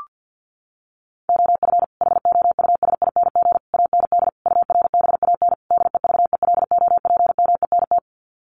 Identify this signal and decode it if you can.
Morse code, heard as a keyed tone, about 50 wpm, 700 Hz. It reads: OF HOUSING AND URBAN DEVELOPMENT
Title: Sets of 5 Words Encore2 50wpm